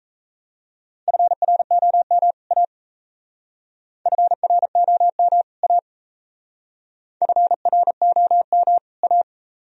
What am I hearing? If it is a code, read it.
Morse code, heard as a keyed tone, about 34 wpm, 690 Hz. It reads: FROM A FROM A FROM A
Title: Speed-Racing - Sets of 2 Words 28wpm